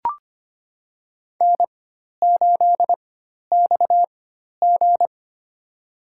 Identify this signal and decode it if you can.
Morse code, heard as a keyed tone, about 25 wpm, spaced as for 12 wpm, 700 Hz. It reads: N8XG
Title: USA Call Signs - 4x Character Spacing - 25wpm